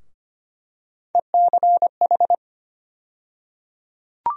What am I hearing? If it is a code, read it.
Morse code, heard as a keyed tone, about 25 wpm, 700 Hz. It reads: ECH
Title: USA Call Sign Postfixes - 3 Letters 25wpm